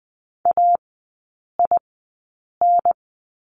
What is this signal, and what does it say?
Morse code, heard as a keyed tone, about 20 wpm, spaced as for 8 wpm, 700 Hz. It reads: AIN
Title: 047 Lesson 9e - Number 4 - Three Characters 20wpm